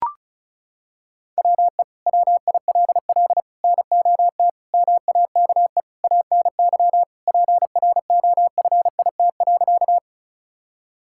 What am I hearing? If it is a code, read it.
Morse code, heard as a keyed tone, about 34 wpm, 700 Hz. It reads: WE WILL NOT MAKE ANY PROFIT.
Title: Sentences from Top 1000 Words - Review Part 5 - 35wpm